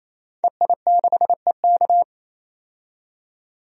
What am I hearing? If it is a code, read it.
Morse code, heard as a keyed tone, about 27 wpm, 700 Hz. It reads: EI6EK
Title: DXCC-100 Call Signs - 5 Characters - 28wpm